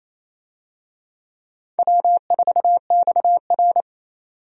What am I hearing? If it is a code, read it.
Morse code, heard as a keyed tone, about 27 wpm, 700 Hz. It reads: W4XR
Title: USA Call Signs - 4 Characters - 28wpm